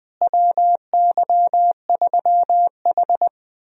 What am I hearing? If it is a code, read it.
Morse code, heard as a keyed tone, about 20 wpm, 700 Hz. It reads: WY3H
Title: USA Call Signs - 4 Characters - Repeated 2x - 20wpm